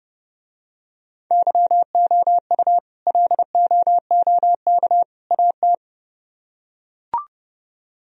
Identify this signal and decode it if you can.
Morse code, heard as a keyed tone, about 27 wpm, 700 Hz. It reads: YOU LOOK AT